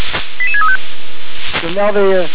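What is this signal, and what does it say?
Five-tone selective call (ZVEI), heard as quick runs of five tones, tones of about 80 ms, with speech. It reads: 88635
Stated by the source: ZVEI five-tone selective call received on an SDR.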